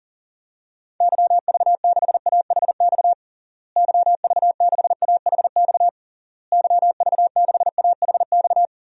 Morse code, heard as a keyed tone, about 40 wpm, 700 Hz. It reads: YV6AHX YV6AHX YV6AHX
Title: DXCC-100 Call Signs - 6 Characters - Repeated 3x - 40wpm